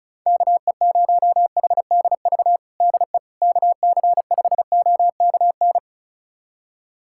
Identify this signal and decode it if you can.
Morse code, heard as a keyed tone, about 36 wpm, 700 Hz. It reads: KE0HDV DE KC5OKN
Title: USA Callsign DE Callsign 35wpm